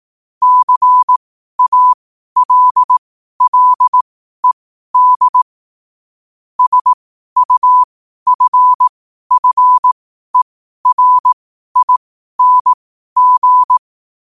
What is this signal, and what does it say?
Morse code, heard as a keyed tone, about 18 wpm, spaced as for 12 wpm, 1000 Hz. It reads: CALLED SUFFERING
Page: New quotes every day in morse code at 12 Words per minute.